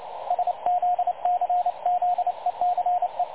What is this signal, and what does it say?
Morse code, heard as a keyed tone, about 27 wpm, 720 Hz, with radio noise. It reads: SZCZECE
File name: dlazaawansowanych / nagraniazpasma / websdr recording 3565.0kHz